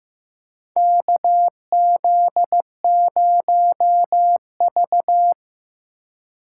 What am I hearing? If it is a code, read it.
Morse code, heard as a keyed tone, about 15 wpm, 700 Hz. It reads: KZ0V
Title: USA Call Signs - 4 Characters - 4x Character Spacing - 15wpm